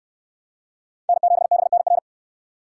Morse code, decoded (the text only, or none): N7BNR